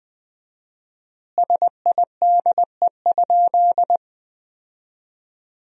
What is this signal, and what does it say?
Morse code, heard as a keyed tone, about 21 wpm, 700 Hz. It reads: SIDE?
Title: 4x Word-Spacing - Sentences from Top 400 Words - 20wpm